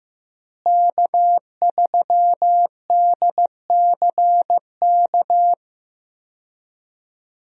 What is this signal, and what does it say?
Morse code, heard as a keyed tone, about 15 wpm, 700 Hz. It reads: K3DCK
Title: USA Call Signs - 5 Characters - Repeated 2x - 15wpm